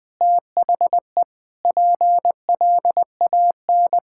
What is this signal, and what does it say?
Morse code, heard as a keyed tone, about 21 wpm, 700 Hz. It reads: THE PLAN
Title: Sentences with 5-Word Phrases - Spoken Beginning - 20wpm